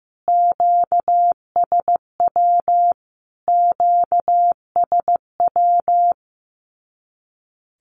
Morse code, heard as a keyed tone, about 15 wpm, 700 Hz. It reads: QSW QSW
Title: Q-Codes Repeated 3x 15wpm